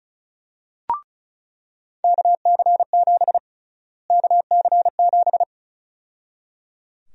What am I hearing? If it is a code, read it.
Morse code, heard as a keyed tone, about 34 wpm, 700 Hz. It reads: KC7 KC7
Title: USA Call Sign Prefixes - Repeated 2x - 35wpm